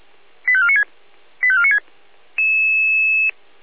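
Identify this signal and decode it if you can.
Five-tone selective call (ZVEI), heard as quick runs of five tones, tones of about 70 ms. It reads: 85496 85496
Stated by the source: Five-tone ZVEI selective call received on SDR.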